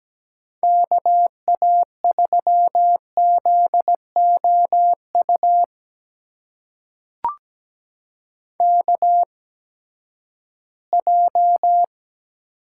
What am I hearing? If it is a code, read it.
Morse code, heard as a keyed tone, about 17 wpm, spaced as for 12 wpm, 700 Hz. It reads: KA3ZOU KJ
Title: USA Call Signs - 8x Character Spacing - 17wpm